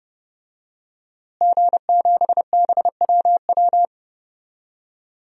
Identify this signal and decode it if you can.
Morse code, heard as a keyed tone, about 32 wpm, 700 Hz. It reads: G7BWW
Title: DXCC-100 Call Signs - 5 Characters - Repeated 2x - 30wpm